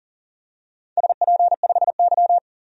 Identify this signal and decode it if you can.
Morse code, heard as a keyed tone, about 40 wpm, 700 Hz. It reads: SP5Y
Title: DXCC-100 Call Signs - 4 Characters - 40wpm